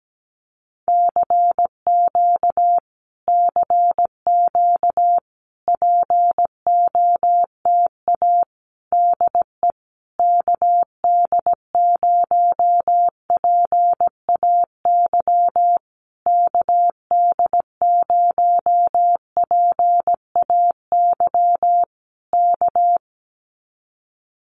Morse code, heard as a keyed tone, about 17 wpm, 700 Hz. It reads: CQ CQ POTA DE KD0PAY KD0PAY K
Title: Common QSO Phrases Encore 17wpm